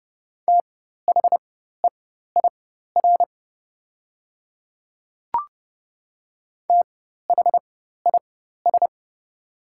Morse code, heard as a keyed tone, about 30 wpm, spaced as for 14 wpm, 700 Hz. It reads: THEIR THIS
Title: 4x Character Spacing - Top 100 Words - 30wpm